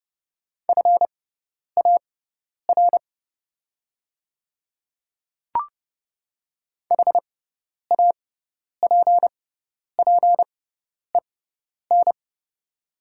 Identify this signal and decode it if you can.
Morse code, heard as a keyed tone, about 30 wpm, spaced as for 10 wpm, 690 Hz. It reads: FAR HAPPEN
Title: 6x Character Spacing - Top 500 Words - 30wpm